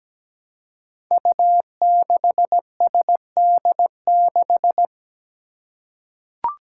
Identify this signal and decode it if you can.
Morse code, heard as a keyed tone, about 17 wpm, 700 Hz. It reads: U6SD6
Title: Five Challenging Letter-Number Combinations 17wpm